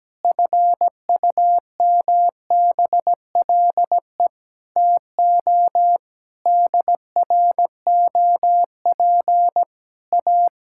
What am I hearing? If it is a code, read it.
Morse code, heard as a keyed tone, about 17 wpm, 700 Hz. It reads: FUMBLE TO DROP A
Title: Pangram Challenge v2 - 17wpm